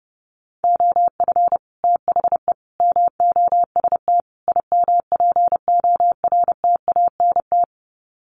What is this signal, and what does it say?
Morse code, heard as a keyed tone, about 30 wpm, 700 Hz. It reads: OF THE MOST IMPORTANT
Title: Sets of 4 Words 30wpm